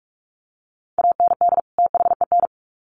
Morse code, heard as a keyed tone, about 44 wpm, 700 Hz. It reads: AND THEN
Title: Sets of 2 Words 45wpm